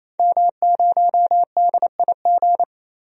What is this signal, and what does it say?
Morse code, heard as a keyed tone, about 28 wpm, 700 Hz. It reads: M0DIG
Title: DXCC-100 Call Signs - 5 Characters - 28wpm